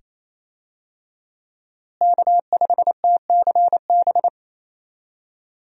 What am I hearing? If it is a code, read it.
Morse code, heard as a keyed tone, about 27 wpm, 700 Hz. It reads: K5TCB
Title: USA Call Signs - 5 Characters - Repeated 2x - 28wpm